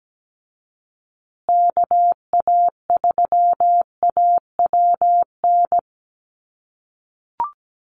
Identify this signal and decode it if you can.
Morse code, heard as a keyed tone, about 17 wpm, 700 Hz. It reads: KA3AWN
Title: USA Call Signs Repeated 2x 17wpm